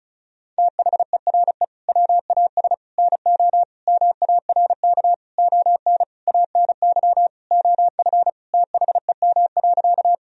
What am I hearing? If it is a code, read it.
Morse code, heard as a keyed tone, about 34 wpm, 700 Hz. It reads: THERE WAS NO MARK ON ANY OF THEM.
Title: Sentences from Top 1000 Words - Review Part 1 - 35wpm